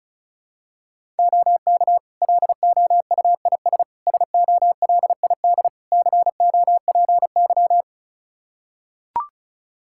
Morse code, heard as a keyed tone, about 36 wpm, 700 Hz. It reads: OK LOUIS SOLID COPY
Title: Practice QSO - Exchange 3 - More Info - 35wpm